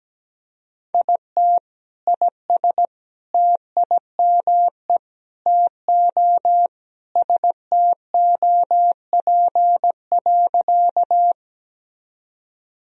Morse code, heard as a keyed tone, about 17 wpm, 700 Hz. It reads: IT IS TIME TO STOP.
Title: Sentences from Top 1000 Words - Review Part 1 - 17wpm